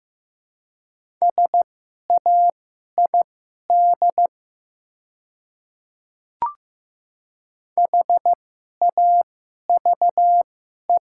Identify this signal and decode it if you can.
Morse code, heard as a keyed tone, about 15 wpm, spaced as for 11 wpm, 700 Hz. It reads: SAID HAVE
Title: 2x Character Spacing - Top 300 Words - 15wpm